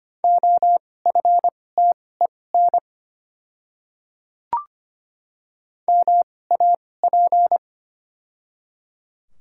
Morse code, heard as a keyed tone, about 25 wpm, spaced as for 18 wpm, 700 Hz. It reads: OFTEN MAP
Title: 2x Character Spacing - Top 500 Words - 25wpm